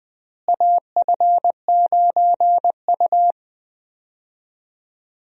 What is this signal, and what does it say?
Morse code, heard as a keyed tone, about 20 wpm, 700 Hz. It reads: AF9U
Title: USA Call Signs - 4 Characters - Repeated 2x - 20wpm